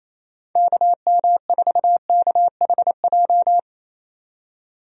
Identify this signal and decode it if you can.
Morse code, heard as a keyed tone, about 27 wpm, 700 Hz. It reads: KM4KHJ
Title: USA Call Signs - 6 Characters - 28wpm